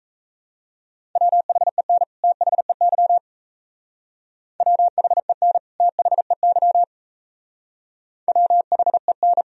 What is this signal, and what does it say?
Morse code, heard as a keyed tone, about 34 wpm, 700 Hz. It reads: WHEN THEY WHEN THEY WHEN
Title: Speed-Racing - Sets of 2 Words 28wpm